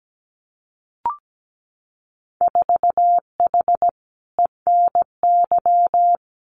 Morse code, heard as a keyed tone, about 17 wpm, 700 Hz. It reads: 4H ENY
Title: Winter Field Day Exchanges Encore 17wpm